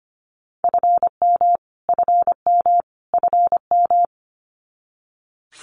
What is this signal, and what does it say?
Morse code, heard as a keyed tone, about 25 wpm, 700 Hz. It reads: FM FM FM
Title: Abbreviations Repeated 3x - 25wpm